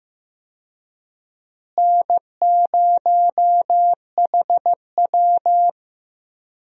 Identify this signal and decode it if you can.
Morse code, heard as a keyed tone, about 15 wpm, 700 Hz. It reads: N0HW
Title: USA Call Signs - 4 Characters - Repeated 2x - 15wpm